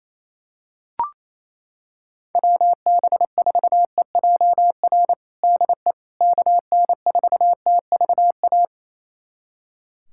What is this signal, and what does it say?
Morse code, heard as a keyed tone, about 28 wpm, 700 Hz. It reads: WB4EJR DE KN4TVA
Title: USA Callsign DE Callsign 28wpm